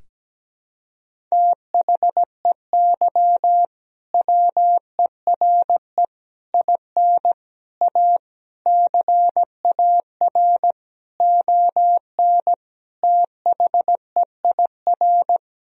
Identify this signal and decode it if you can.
Morse code, heard as a keyed tone, about 17 wpm, 700 Hz. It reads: THEY WERE IN A CAR ON THEIR
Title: Sentences from Top 1000 Words - Review Part 1 - 17wpm